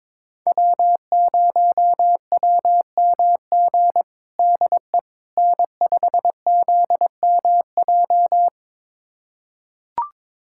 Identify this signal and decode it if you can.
Morse code, heard as a keyed tone, about 22 wpm, 700 Hz. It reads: W0WMG DE N5ZMJ
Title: USA Callsign DE Callsign - 5 Characters - 22wpm